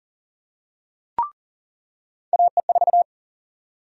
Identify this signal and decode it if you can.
Morse code, heard as a keyed tone, about 39 wpm, 700 Hz. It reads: AE4
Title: USA Call Sign Prefixes - 40wpm